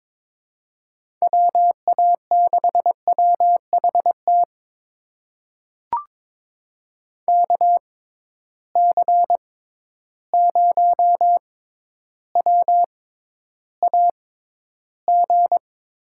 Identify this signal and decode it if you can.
Morse code, heard as a keyed tone, about 22 wpm, spaced as for 11 wpm, 700 Hz. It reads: WA6WHT KC0WAG
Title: USA Call Signs - 6 Characters - 6x Character Spacing - 22wpm